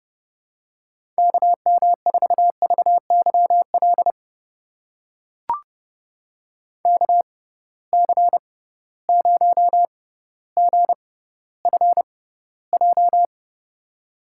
Morse code, heard as a keyed tone, about 30 wpm, spaced as for 15 wpm, 700 Hz. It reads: KM4VYL KC0GFJ
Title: USA Call Signs - 6 Characters - 6x Character Spacing - 30wpm